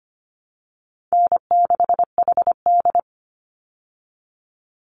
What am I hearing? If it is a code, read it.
Morse code, heard as a keyed tone, about 25 wpm, 700 Hz. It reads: N6HD
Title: Easy Call Signs Repeated 2x - 25wpm